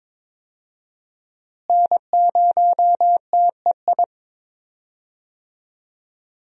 Morse code, heard as a keyed tone, about 22 wpm, 700 Hz. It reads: N0TEI
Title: USA Call Signs - 5 Characters - Repeated 3x - 22wpm